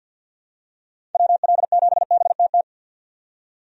Morse code, heard as a keyed tone, about 50 wpm, 700 Hz. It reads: WL7BTT
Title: USA Call Signs - 6 Characters - 50wpm